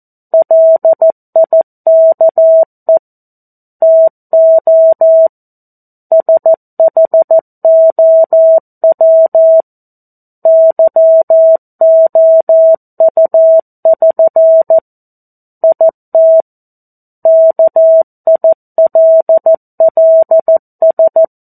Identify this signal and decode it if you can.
Morse code, heard as a keyed tone, about 14 wpm, 650 Hz. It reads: LIKE TO SHOW YOU<SN> IT KILLS
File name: 02-Chapter-02 14 WPM